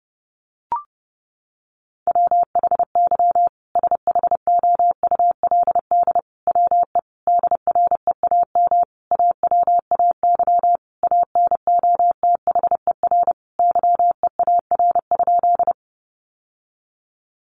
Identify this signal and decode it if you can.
Morse code, heard as a keyed tone, about 28 wpm, 700 Hz. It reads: WHY SHOULD WE DREAM AWAY ANOTHER YEAR?
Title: Sentences from Top 1500 Words 30wpm